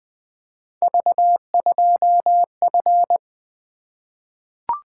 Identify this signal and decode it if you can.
Morse code, heard as a keyed tone, about 20 wpm, 700 Hz. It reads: V2F
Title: Three Letter-Number Combinations - 20wpm